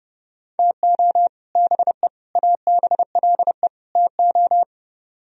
Morse code, heard as a keyed tone, about 32 wpm, 700 Hz. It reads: TO BE ABLE TO